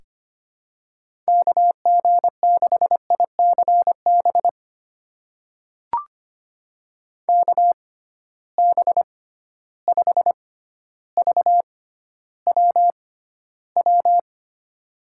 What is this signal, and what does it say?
Morse code, heard as a keyed tone, about 25 wpm, spaced as for 13 wpm, 700 Hz. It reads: KG6ICB KB5VWW
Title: USA Call Signs - 6x Character Spacing - 25wpm